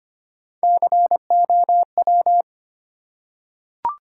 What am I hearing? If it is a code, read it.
Morse code, heard as a keyed tone, about 25 wpm, 700 Hz. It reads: COW
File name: practice / 25wpm / Easy Words with Hints - 25wpm